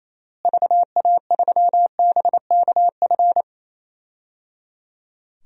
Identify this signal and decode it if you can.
Morse code, heard as a keyed tone, about 28 wpm, 700 Hz. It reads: VA3BKF
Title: DXCC-100 Call Signs - 6 Characters - Repeated 3x - 28wpm